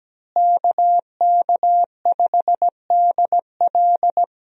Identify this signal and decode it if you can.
Morse code, heard as a keyed tone, about 17 wpm, 700 Hz. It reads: KK5DL
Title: USA Call Signs - 5 Characters - Repeated 3x - 17wpm